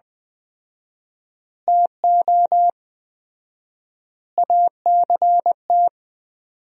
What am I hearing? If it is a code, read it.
Morse code, heard as a keyed tone, about 20 wpm, 700 Hz. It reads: TO ACT
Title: 4x Word-Spacing - Sentences from Top 500 Words - 20wpm